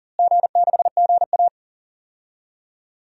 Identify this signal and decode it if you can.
Morse code, heard as a keyed tone, about 40 wpm, 700 Hz. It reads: G6GA
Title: DXCC-100 Call Signs - 4 Characters - Repeated 2x - 40wpm